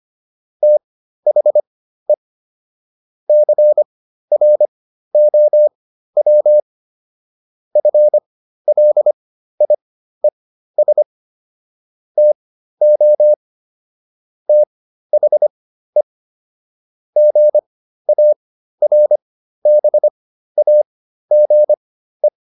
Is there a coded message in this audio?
Yes: Morse code from 0.6 to 22.3 s, heard as a keyed tone, about 25 wpm, spaced as for 13 wpm, 600 Hz. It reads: THE CROW FLIES TO THE GARBAGE